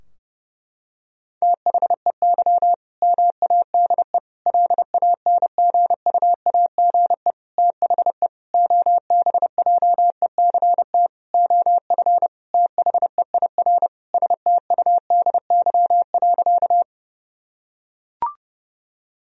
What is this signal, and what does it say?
Morse code, heard as a keyed tone, about 32 wpm, 700 Hz. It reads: THEY MADE LANGUAGE THE OBJECT OF THEIR STUDY.
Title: Sentences from Top 1000 Words - Review Part 3 - 30wpm